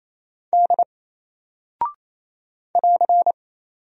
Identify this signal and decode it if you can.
Morse code, heard as a keyed tone, about 27 wpm, 700 Hz. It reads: D <AR>
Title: Single Character (Letter:Number:Punctuation:Prosign) Alt 28wpm